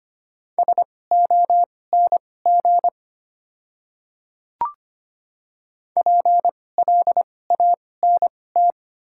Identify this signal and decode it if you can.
Morse code, heard as a keyed tone, about 25 wpm, spaced as for 18 wpm, 700 Hz. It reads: SONG PLANT
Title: 2x Character Spacing - Top 500 Words - 25wpm